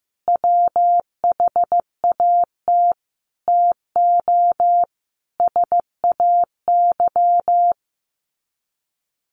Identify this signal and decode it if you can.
Morse code, heard as a keyed tone, about 15 wpm, 700 Hz. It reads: WHAT TO SAY
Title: Sets of 5 Words 15wpm